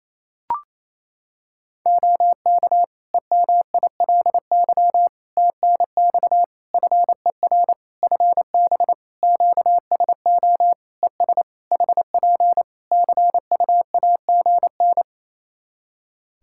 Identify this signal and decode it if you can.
Morse code, heard as a keyed tone, about 27 wpm, 700 Hz. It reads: OK EMILY TNX FER FB QSO ES HP CUAGN
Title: Practice QSO - Exchange 4 - Ending - 28wpm